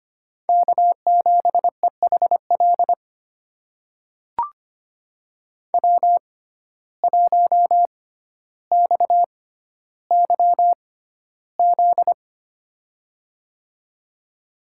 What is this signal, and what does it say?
Morse code, heard as a keyed tone, about 25 wpm, spaced as for 13 wpm, 690 Hz. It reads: K7EHL W1XYZ
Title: USA Call Signs - 5 Characters - 6x Character Spacing - 25wpm